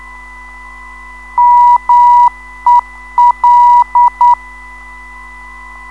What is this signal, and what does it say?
Morse code, heard as a keyed tone, about 9 wpm, 970 Hz. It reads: MEL
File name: ndb zvok / mel-ron-mel